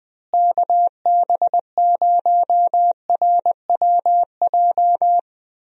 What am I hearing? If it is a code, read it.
Morse code, heard as a keyed tone, about 21 wpm, 700 Hz. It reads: KB0RWJ
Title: USA Call Signs - 6 Characters - Repeated 2x - 20wpm